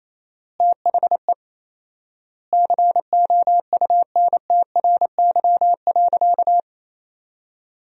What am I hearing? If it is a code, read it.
Morse code, heard as a keyed tone, about 27 wpm, 700 Hz. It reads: THE COUNTRY.
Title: 4x Word-Spacing - Sentences from Top 300 Words - 28wpm